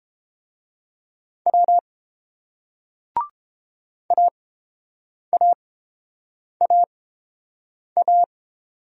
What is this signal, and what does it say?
Morse code, heard as a keyed tone, about 30 wpm, spaced as for 7 wpm, 690 Hz. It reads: W AAAA